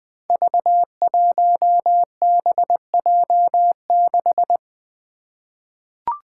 Morse code, heard as a keyed tone, about 20 wpm, 700 Hz. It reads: V1BJ6
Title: Five Challenging Letter-Number Combinations 20wpm